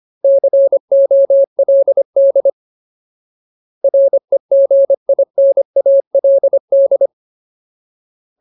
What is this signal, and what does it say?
Morse code, heard as a keyed tone, about 25 wpm, 550 Hz. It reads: COLD REGINALD
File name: Ultimate Word List / 25WPM Ultimate